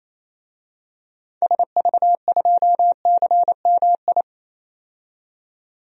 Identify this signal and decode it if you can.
Morse code, heard as a keyed tone, about 28 wpm, 700 Hz. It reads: SV2CMI
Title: DXCC-100 Call Signs - 6 Characters - 28wpm